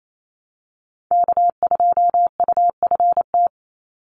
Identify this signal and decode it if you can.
Morse code, heard as a keyed tone, about 28 wpm, 700 Hz. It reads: K2UFT
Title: Call Signs Repeated 3x - 28wpm